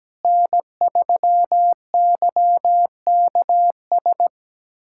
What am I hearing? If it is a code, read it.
Morse code, heard as a keyed tone, about 17 wpm, 700 Hz. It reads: N3YKS
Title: USA Callsign DE Callsign - 5 Characters - 17wpm